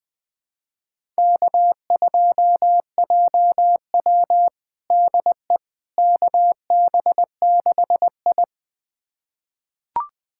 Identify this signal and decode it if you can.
Morse code, heard as a keyed tone, about 20 wpm, 700 Hz. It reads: K2JW DE KB6I
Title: USA Callsign DE Callsign - 4 Characters - 20wpm